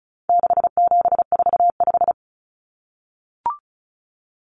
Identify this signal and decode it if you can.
Morse code, heard as a keyed tone, about 34 wpm, 700 Hz. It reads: 6745
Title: Numbers - 4-Digit 35wpm